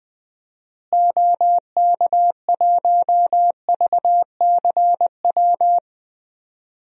Morse code, heard as a keyed tone, about 20 wpm, 700 Hz. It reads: OK1VCW
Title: DXCC-100 Call Signs - 6 Characters - Repeated 3x - 20wpm